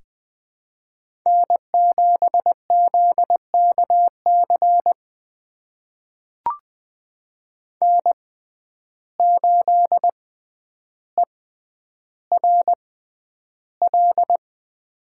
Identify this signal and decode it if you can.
Morse code, heard as a keyed tone, about 21 wpm, spaced as for 10 wpm, 700 Hz. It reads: N7ZKC N8ERL
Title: USA Call Signs - 5 Characters - 6x Character Spacing - 20wpm